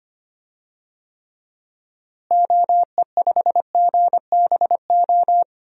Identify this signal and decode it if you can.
Morse code, heard as a keyed tone, about 25 wpm, 700 Hz. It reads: OE5GBO
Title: DXCC-100 Call Signs - 6 Characters - Repeated 2x - 25wpm